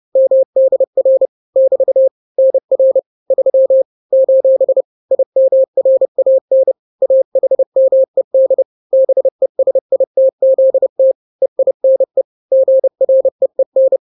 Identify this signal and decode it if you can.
Morse code, heard as a keyed tone, about 30 wpm, 530 Hz. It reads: MDR <BT> NR 3 : IMRAN AHMED BESITZT EINE GREEN
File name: 29wpm